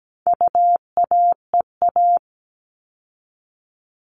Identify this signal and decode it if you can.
Morse code, heard as a keyed tone, about 17 wpm, 700 Hz. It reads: UAEA
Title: All Cut Numbers - 4-Digit 17wpm